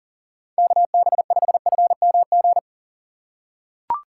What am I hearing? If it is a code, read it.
Morse code, heard as a keyed tone, about 40 wpm, 700 Hz. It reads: KB5FMG
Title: USA Call Signs - 6 Characters - 8x Character Spacing - 40wpm